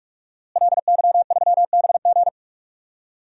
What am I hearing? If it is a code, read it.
Morse code, heard as a keyed tone, about 46 wpm, 700 Hz. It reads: LY3BG
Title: DXCC-100 Call Signs - 5 Characters - Repeated 2x - 45wpm